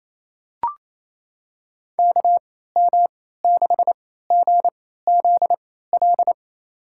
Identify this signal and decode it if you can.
Morse code, heard as a keyed tone, about 28 wpm, spaced as for 16 wpm, 700 Hz. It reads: KM6GZL
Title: USA Call Signs - 6 Characters - 3x Character Spacing - 28wpm